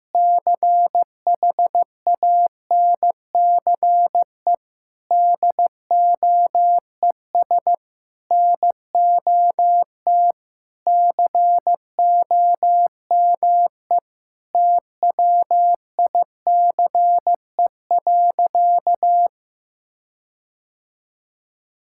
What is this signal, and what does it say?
Morse code, heard as a keyed tone, about 16 wpm, 700 Hz. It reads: CHANCE DOES NOT COME TWICE.
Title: Sentences from Top 1600 Words 15wpm